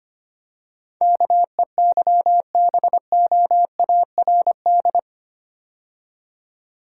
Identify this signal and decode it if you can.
Morse code, heard as a keyed tone, about 25 wpm, 700 Hz. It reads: KEYBOARD
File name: 8 Letter Words - First 3 Letters Given - 25wpm